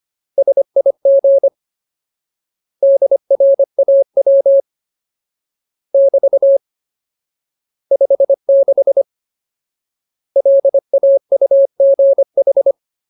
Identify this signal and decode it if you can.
Morse code, heard as a keyed tone, about 25 wpm, 550 Hz. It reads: SIG DRAW <BT> 56 LAUGH
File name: Ultimate Word List / 25WPM Ultimate